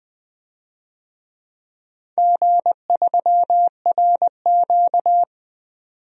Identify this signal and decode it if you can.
Morse code, heard as a keyed tone, about 20 wpm, 700 Hz. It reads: G3RQ
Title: DXCC-100 Call Signs - 4 Characters - 20wpm